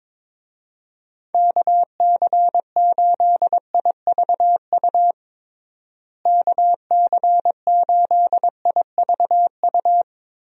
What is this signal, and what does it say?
Morse code, heard as a keyed tone, about 22 wpm, 700 Hz. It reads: KC8IVU KC8IVU
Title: USA Call Signs - 6 Characters - Repeated 3x - 22wpm